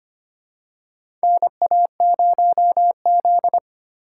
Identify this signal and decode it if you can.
Morse code, heard as a keyed tone, about 25 wpm, 700 Hz. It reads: NA0Z
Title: USA Call Signs - 4 Characters - 2x Character Spacing - 25wpm